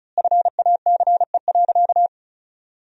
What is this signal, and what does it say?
Morse code, heard as a keyed tone, about 36 wpm, 700 Hz. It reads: FACE.
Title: 4x Word-Spacing - Sentences from Top 300 Words - 35wpm